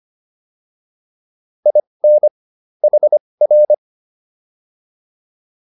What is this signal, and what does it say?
Morse code, heard as a keyed tone, about 25 wpm, spaced as for 20 wpm, 600 Hz. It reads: IN HR